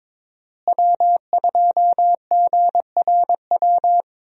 Morse code, heard as a keyed tone, about 22 wpm, 700 Hz. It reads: W2GRW
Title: USA Call Signs - 5 Characters - Repeated 3x - 22wpm